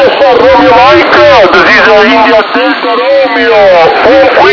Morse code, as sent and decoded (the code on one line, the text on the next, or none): none